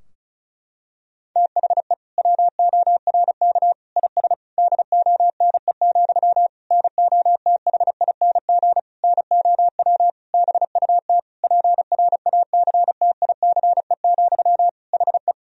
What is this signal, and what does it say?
Morse code, heard as a keyed tone, about 36 wpm, 700 Hz. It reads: THE WORK IS DONE, NOTHING NOW BUT PRACTICE, HE
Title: Sentences from Top 1000 Words - Review Part 4 - 35wpm